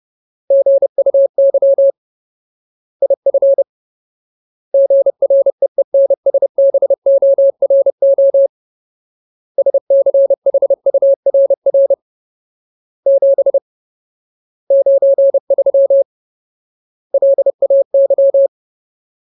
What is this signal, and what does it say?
Morse code, heard as a keyed tone, about 30 wpm, 550 Hz. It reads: GUY IF GREENSBORO SCHURR 7 93 LAY
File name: Ultimate Word List / 30WPM Ultimate